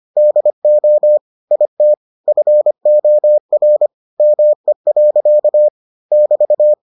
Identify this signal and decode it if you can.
Morse code, heard as a keyed tone, about 25 wpm, 600 Hz. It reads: DO IT FOR ME. <BT>